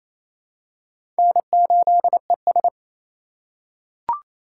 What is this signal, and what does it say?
Morse code, heard as a keyed tone, about 28 wpm, 700 Hz. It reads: N8ES
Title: USA Call Signs - 4 Characters - 2x Character Spacing - 28wpm